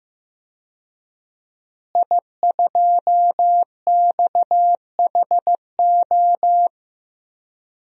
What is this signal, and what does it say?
Morse code, heard as a keyed tone, about 15 wpm, 700 Hz. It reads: I2XHO
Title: DXCC-100 Call Signs - 5 Characters - Repeated 3x - 15wpm